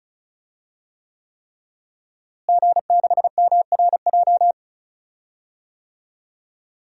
Morse code, heard as a keyed tone, about 34 wpm, 700 Hz. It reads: G6MRJ
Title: DXCC-100 Call Signs - 5 Characters - Repeated 3x - 35wpm